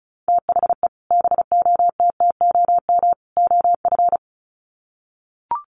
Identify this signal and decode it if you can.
Morse code, heard as a keyed tone, about 34 wpm, 700 Hz. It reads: THE BOTTOM OF
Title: Sets of 3 Words 35wpm